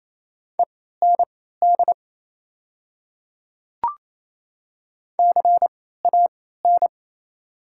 Morse code, heard as a keyed tone, about 28 wpm, spaced as for 16 wpm, 690 Hz. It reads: END CAN
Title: 3x Character Spacing - Top 200 Words - 28wpm